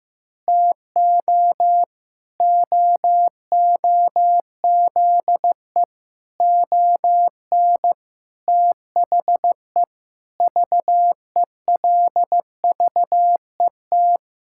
Morse code, heard as a keyed tone, about 15 wpm, 700 Hz. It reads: TO OOZE ON THE VELVET
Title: Pangram Challenge v2 - 15wpm